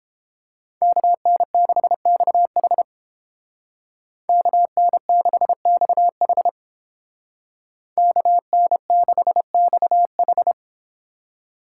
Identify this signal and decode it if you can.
Morse code, heard as a keyed tone, about 29 wpm, 700 Hz. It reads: KN6XH KN6XH KN6XH